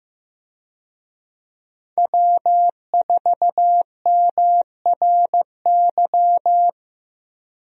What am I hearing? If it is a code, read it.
Morse code, heard as a keyed tone, about 15 wpm, 700 Hz. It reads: W4MRY
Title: USA Call Signs - 5 Characters - Repeated 2x - 15wpm